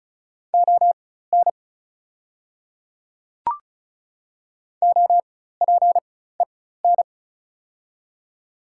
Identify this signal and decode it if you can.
Morse code, heard as a keyed tone, about 36 wpm, spaced as for 16 wpm, 690 Hz. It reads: ON OPEN
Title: 4x Character Spacing - Top 500 Words - 35wpm